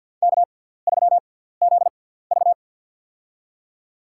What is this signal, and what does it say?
Morse code, heard as a keyed tone, about 53 wpm, spaced as for 17 wpm, 700 Hz. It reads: K3ZV
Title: USA Call Signs - 6x Character Spacing - 50wpm